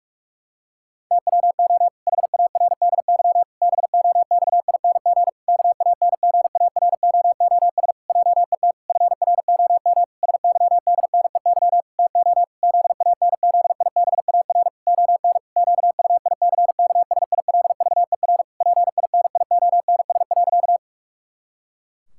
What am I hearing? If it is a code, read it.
Morse code, heard as a keyed tone, about 41 wpm, 700 Hz. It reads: TWO HARDY BOXING KANGAROOS JET FROM SYDNEY TO ZANZIBAR ON QUICKSILVER PINIONS.
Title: Pangram Challenge v2 - 45wpm